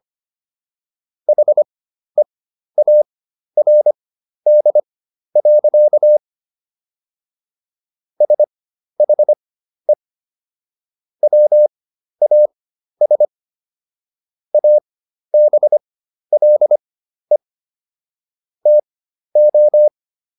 Morse code, heard as a keyed tone, about 25 wpm, spaced as for 12 wpm, 600 Hz. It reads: HEARD. SHE WAS ABLE TO